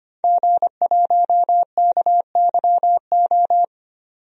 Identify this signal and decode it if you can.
Morse code, heard as a keyed tone, about 25 wpm, 700 Hz. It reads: G1KYO
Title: DXCC-100 Call Signs - 5 Characters - 25wpm